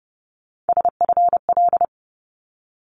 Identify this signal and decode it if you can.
Morse code, heard as a keyed tone, about 30 wpm, 700 Hz. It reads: SFL
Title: ARRL Sections 30wpm